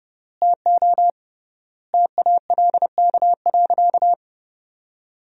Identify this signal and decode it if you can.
Morse code, heard as a keyed tone, about 30 wpm, 700 Hz. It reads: TO TALK.
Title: 3x Word-Spacing - Sentences from Top 400 Words - 30wpm